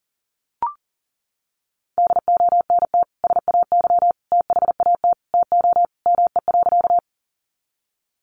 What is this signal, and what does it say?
Morse code, heard as a keyed tone, about 40 wpm, 700 Hz. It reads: DONT SAY THAT TO ME.
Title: Sentences from Top 300 Words 40wpm